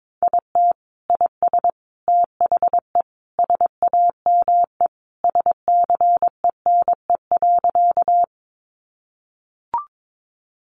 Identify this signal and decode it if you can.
Morse code, heard as a keyed tone, about 22 wpm, 700 Hz. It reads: IT IS THE SAME SCENE.
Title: Sentences from Top 1100 Words 22wpm